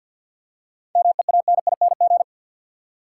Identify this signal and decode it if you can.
Morse code, heard as a keyed tone, about 53 wpm, 700 Hz. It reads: MEANING
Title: Seven Letter Words - 50wpm